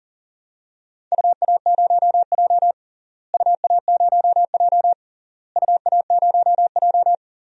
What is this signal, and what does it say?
Morse code, heard as a keyed tone, about 37 wpm, 700 Hz. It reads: UA0J UA0J UA0J
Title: DXCC-100 Call Signs - 4 Characters - Repeated 3x - 40wpm